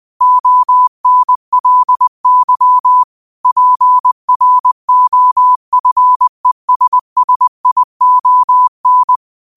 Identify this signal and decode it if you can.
Morse code, heard as a keyed tone, about 20 wpm, 1000 Hz. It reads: ONLY PROFESSION
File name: qotdfiles / 20wpm